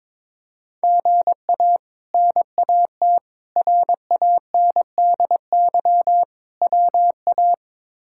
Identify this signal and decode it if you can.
Morse code, heard as a keyed tone, about 22 wpm, 700 Hz. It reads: GA NAT RANDY WA
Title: K1USN SST QSO Practice - 22wpm